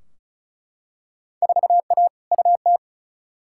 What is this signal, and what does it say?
Morse code, heard as a keyed tone, about 34 wpm, 700 Hz. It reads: 4A UT
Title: Field Day Exchanges - 35wpm